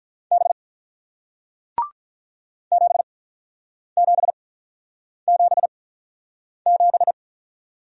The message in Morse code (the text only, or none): B 7777